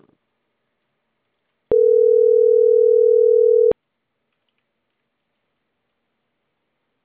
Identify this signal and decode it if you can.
Ringback tone (North America), 440 + 480 Hz.